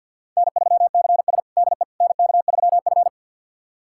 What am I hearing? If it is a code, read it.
Morse code, heard as a keyed tone, about 52 wpm, 700 Hz. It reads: N3CS DE NK3F
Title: USA Callsign DE Callsign - 4 Characters - 50wpm